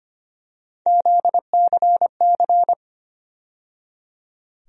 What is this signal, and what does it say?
Morse code, heard as a keyed tone, about 25 wpm, 700 Hz. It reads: ZCC
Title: USA Call Sign Postfixes - 3 Letters 25wpm